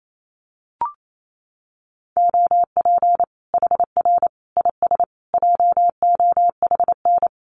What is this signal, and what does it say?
Morse code, heard as a keyed tone, about 28 wpm, 700 Hz. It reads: OP HR IS JOHN
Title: Common QSO Phrases 28wpm